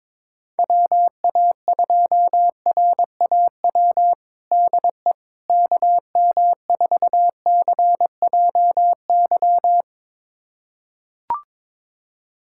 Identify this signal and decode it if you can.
Morse code, heard as a keyed tone, about 22 wpm, 700 Hz. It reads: WA2RAW DE KM4CJY
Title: USA Callsign DE Callsign - 6 Characters - 22wpm